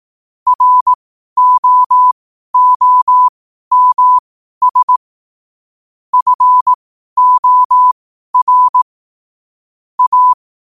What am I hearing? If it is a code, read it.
Morse code, heard as a keyed tone, about 18 wpm, spaced as for 12 wpm, 1000 Hz. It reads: ROOMS FOR A
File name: qotdfiles / 12wpm